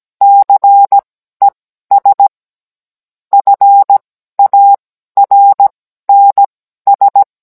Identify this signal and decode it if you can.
Morse code, heard as a keyed tone, about 17 wpm, spaced as for 12 wpm, 800 Hz. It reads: CES FARNS
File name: Cours-NR19-16mm